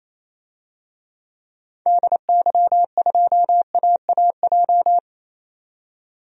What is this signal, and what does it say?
Morse code, heard as a keyed tone, about 28 wpm, 700 Hz. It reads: DY2AAJ
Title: DXCC-100 Call Signs - 6 Characters - 28wpm